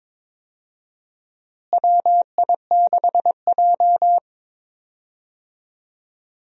Morse code, heard as a keyed tone, about 22 wpm, 700 Hz. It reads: WI6J
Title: USA Call Signs - 4 Characters - Repeated 2x - 22wpm